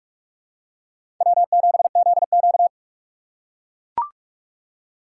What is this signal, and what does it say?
Morse code, heard as a keyed tone, about 44 wpm, 700 Hz. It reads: W7ZQ
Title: USA Call Signs - 4 Characters - 8x Character Spacing - 45wpm